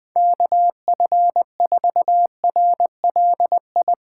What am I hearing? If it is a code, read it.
Morse code, heard as a keyed tone, about 20 wpm, 700 Hz. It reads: KF4RLI
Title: Speed Racing - USA Call Signs 20wpm